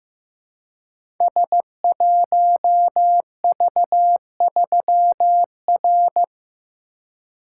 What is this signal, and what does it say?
Morse code, heard as a keyed tone, about 15 wpm, 700 Hz. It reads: S1V3R
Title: Code Groups - 5 Letter:Numbers - 15wpm